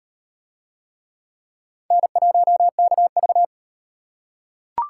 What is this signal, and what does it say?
Morse code, heard as a keyed tone, about 33 wpm, 700 Hz. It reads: N1KV